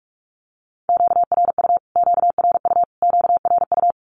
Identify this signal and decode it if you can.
Morse code, heard as a keyed tone, about 44 wpm, 700 Hz. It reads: QRU QRU QRU
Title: Q-Codes Repeated 3x 45wpm